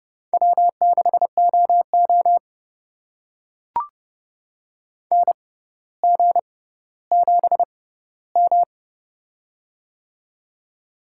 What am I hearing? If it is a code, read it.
Morse code, heard as a keyed tone, about 30 wpm, spaced as for 15 wpm, 700 Hz. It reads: W6OO NG7M
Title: USA Call Signs - 4 Characters - 6x Character Spacing - 30wpm